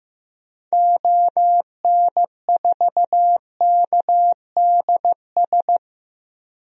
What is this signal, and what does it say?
Morse code, heard as a keyed tone, about 15 wpm, 700 Hz. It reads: ON4KDS
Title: DXCC-100 Call Signs - 6 Characters - Repeated 3x - 15wpm